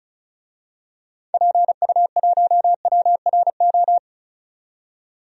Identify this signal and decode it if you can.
Morse code, heard as a keyed tone, about 36 wpm, 690 Hz. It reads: PU1WRO
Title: DXCC-100 Call Signs - 6 Characters - 35wpm